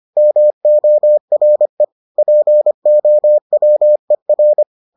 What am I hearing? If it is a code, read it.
Morse code, heard as a keyed tone, about 25 wpm, 600 Hz. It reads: MORE POWER